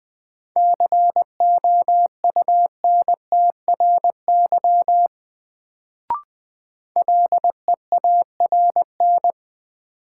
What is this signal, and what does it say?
Morse code, heard as a keyed tone, about 20 wpm, 700 Hz. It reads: COUNTRY LEARN
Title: Sentences from Top 300 Words - New Word Drill - 20wpm